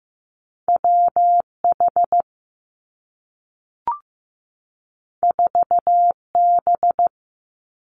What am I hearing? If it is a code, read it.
Morse code, heard as a keyed tone, about 15 wpm, 700 Hz. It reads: W H 4 B
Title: Two Challenging Letter-Number Combinations 15wpm